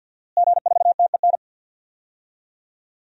Morse code, heard as a keyed tone, about 52 wpm, 700 Hz. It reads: G4TEN
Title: DXCC-100 Call Signs - 5 Characters - Repeated 2x - 50wpm